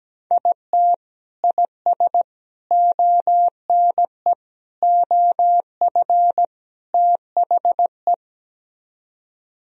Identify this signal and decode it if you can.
Morse code, heard as a keyed tone, about 17 wpm, 700 Hz. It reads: IT IS ONE OF THE
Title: Speed-Racing - Sets of 5 Words 17wpm